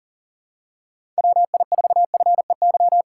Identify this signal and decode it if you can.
Morse code, heard as a keyed tone, about 38 wpm, 700 Hz. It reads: WI4FEY
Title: USA Call Signs - 6 Characters - 4x Character Spacing - 40wpm